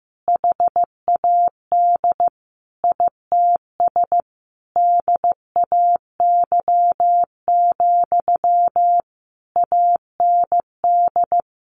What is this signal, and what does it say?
Morse code, heard as a keyed tone, about 14 wpm, 700 Hz. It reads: HAD ITS DAY, AND
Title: Sentences from Top 1500 Words 15wpm